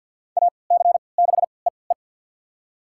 Morse code, heard as a keyed tone, about 51 wpm, spaced as for 29 wpm, 700 Hz. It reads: AC6EE
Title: USA Call Signs - 3x Character Spacing - 50wpm